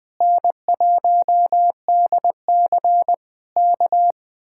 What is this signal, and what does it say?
Morse code, heard as a keyed tone, about 20 wpm, 700 Hz. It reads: N1DC K
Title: Speed-Racing Common QSO Phrases - 17wpm